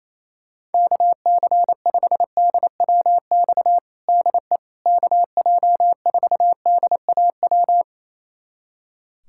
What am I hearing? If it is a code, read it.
Morse code, heard as a keyed tone, about 28 wpm, 700 Hz. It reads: KC5DWX DE KJ4DAW
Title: USA Callsign DE Callsign - 6 Characters - 28wpm